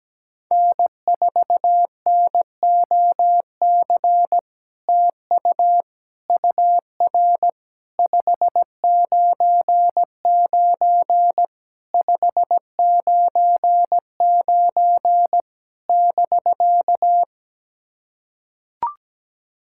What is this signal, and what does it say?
Morse code, heard as a keyed tone, about 17 wpm, 700 Hz. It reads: N4NOC TU UR 599 599 <BK>
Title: SOTA QSO Practice 17wpm